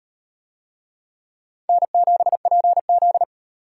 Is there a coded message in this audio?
Yes: Morse code from 1.7 to 3.2 s, heard as a keyed tone, about 38 wpm, 700 Hz. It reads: N7PZ